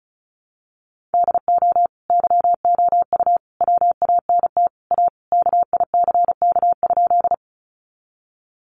Morse code, heard as a keyed tone, about 34 wpm, 700 Hz. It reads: DO YOU WANT A KICK?
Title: Sentences from Top 1300 Words 35wpm